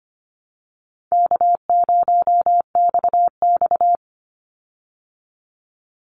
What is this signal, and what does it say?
Morse code, heard as a keyed tone, about 25 wpm, 700 Hz. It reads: K0XX
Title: Easy Call Signs Repeated 3x - 25wpm